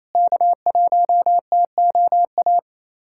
Morse code, heard as a keyed tone, about 28 wpm, 700 Hz. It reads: K1TOA
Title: USA Callsign DE Callsign - 5 Characters - 28wpm